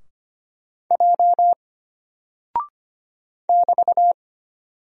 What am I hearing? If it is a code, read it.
Morse code, heard as a keyed tone, about 25 wpm, 700 Hz. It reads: J <BT>
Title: Single Character (Letter:Number:Punctuation:Prosign) Alt 25wpm